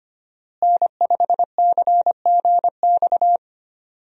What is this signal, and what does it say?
Morse code, heard as a keyed tone, about 25 wpm, 700 Hz. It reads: N5CGX
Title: USA Call Signs - 5 Characters - Repeated 3x - 25wpm